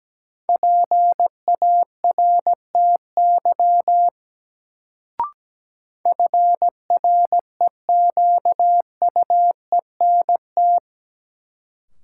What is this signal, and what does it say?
Morse code, heard as a keyed tone, about 17 wpm, 700 Hz. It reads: PARTY FREQUENT
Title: Sentences from Top 1400 Words - New Word Drill Alt - 17wpm